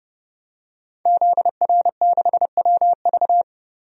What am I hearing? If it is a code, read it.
Morse code, heard as a keyed tone, about 31 wpm, 690 Hz. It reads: ZR6WV
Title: DXCC-100 Call Signs - 5 Characters - 30wpm